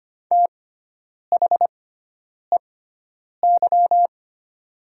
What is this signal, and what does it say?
Morse code, heard as a keyed tone, about 25 wpm, spaced as for 9 wpm, 700 Hz. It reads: THEY
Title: 6x Character Spacing - Top 200 Words - 25wpm